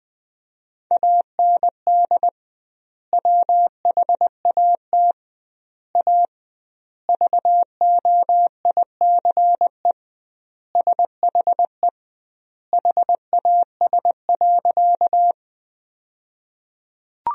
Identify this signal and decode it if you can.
Morse code, heard as a keyed tone, about 21 wpm, 700 Hz. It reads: AND WHAT A VOICE SHE HAS.
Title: 2x Word-Spacing - Sentences from Top 500 Words - 20wpm